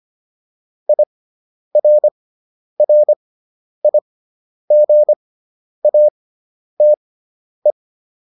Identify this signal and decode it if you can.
Morse code, heard as a keyed tone, about 25 wpm, spaced as for 10 wpm, 600 Hz. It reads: IRRIGATE